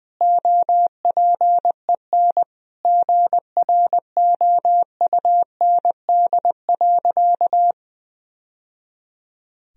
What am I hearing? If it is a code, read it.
Morse code, heard as a keyed tone, about 20 wpm, 700 Hz. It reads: OPEN GROUND.
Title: Sentences from Top 1000 Words - Review Part 2 - 20wpm